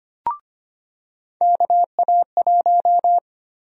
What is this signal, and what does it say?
Morse code, heard as a keyed tone, about 25 wpm, 700 Hz. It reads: KA1
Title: USA Call Sign Prefixes - 25wpm